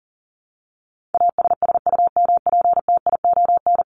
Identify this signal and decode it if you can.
Morse code, heard as a keyed tone, about 38 wpm, 700 Hz. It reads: ASSUMPTION
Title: Top 1751-2000 Words 40wpm